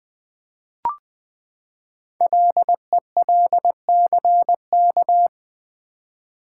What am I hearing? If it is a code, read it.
Morse code, heard as a keyed tone, about 20 wpm, 700 Hz. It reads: LELCK
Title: Code Groups - 5 Letters - 20wpm